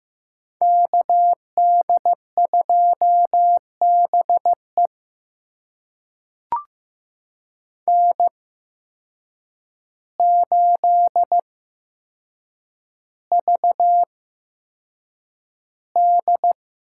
Morse code, heard as a keyed tone, about 15 wpm, spaced as for 7 wpm, 700 Hz. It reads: KD2BE N8VD
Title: USA Call Signs - 5 Characters - 8x Character Spacing - 15wpm